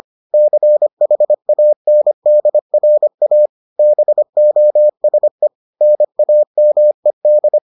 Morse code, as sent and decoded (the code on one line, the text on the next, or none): -.-. .... .- -. -.. .-. .- / -... --- ... / -. .- -- . -..
CHANDRA BOSE NAMED